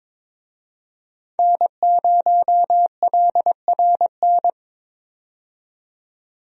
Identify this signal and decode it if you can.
Morse code, heard as a keyed tone, about 22 wpm, 700 Hz. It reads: N0LRN
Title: USA Call Signs - 5 Characters - 22wpm